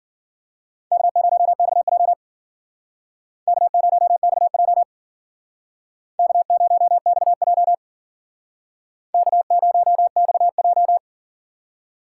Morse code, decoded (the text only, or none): K0XJ K0XJ K0XJ K0XJ